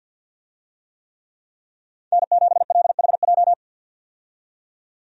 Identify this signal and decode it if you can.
Morse code, heard as a keyed tone, about 52 wpm, 700 Hz. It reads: N7LHJ
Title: USA Call Signs - 5 Characters - 50wpm